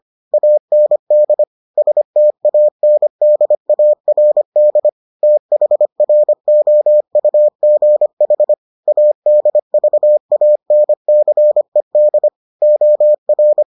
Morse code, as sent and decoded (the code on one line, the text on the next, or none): .- -. -.. / ... - .- -. -.. .- .-. -.. / - .... .-. --- ..- --. .... / .- -.. ...- .- -. -.-. -.. / --- .-.
AND STANDARD THROUGH ADVANCED OR